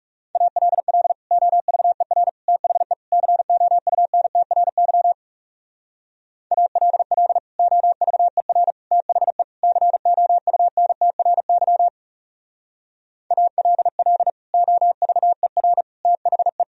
Morse code, heard as a keyed tone, about 40 wpm, 700 Hz. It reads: ALL OVER THE COUNTRY ALL OVER THE COUNTRY ALL OVER THE